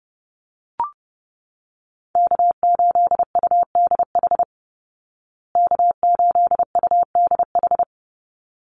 Morse code, heard as a keyed tone, about 30 wpm, 700 Hz. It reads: K8UDH K8UDH
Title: Call Signs Repeated 3x - 30wpm